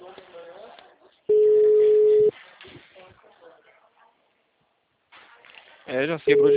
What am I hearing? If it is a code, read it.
Ringback tone (Europe, 425 Hz), with speech.